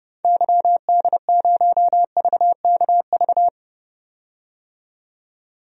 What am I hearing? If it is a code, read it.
Morse code, heard as a keyed tone, about 31 wpm, 700 Hz. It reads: YD0VKV
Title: DXCC-100 Call Signs - 6 Characters - 30wpm